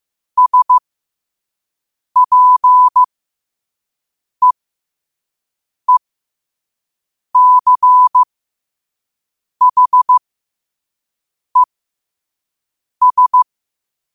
Morse code, heard as a keyed tone, about 15 wpm, spaced as for 5 wpm, 1000 Hz. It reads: SPEECHES
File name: qotdfiles / 5wpm